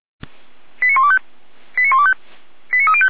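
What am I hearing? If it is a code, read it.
Five-tone selective call (ZVEI), heard as quick runs of five tones, tones of about 70 ms. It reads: 89125 89125 89305
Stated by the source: ZVEI five-tone selective call received by SDR.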